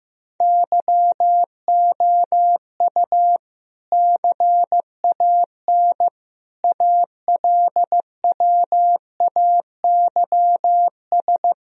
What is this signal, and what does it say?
Morse code, heard as a keyed tone, about 15 wpm, 700 Hz. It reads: YOU CAN ALWAYS
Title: Sentences from Top 1000 Words - Review Part 4 - 15wpm